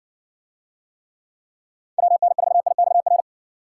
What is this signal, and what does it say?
Morse code, heard as a keyed tone, about 56 wpm, 700 Hz. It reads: WN4IXR